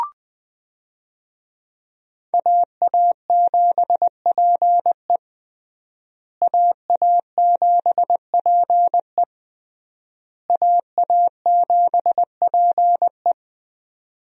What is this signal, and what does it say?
Morse code, heard as a keyed tone, about 20 wpm, 700 Hz. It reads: AA7PE AA7PE AA7PE
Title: USA Call Signs - 5 Characters - Repeated 3x - 20wpm